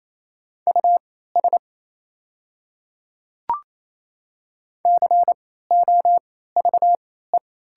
Morse code, heard as a keyed tone, about 28 wpm, spaced as for 16 wpm, 700 Hz. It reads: US COVE
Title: 3x Character Spacing - Top 300 Words - 28wpm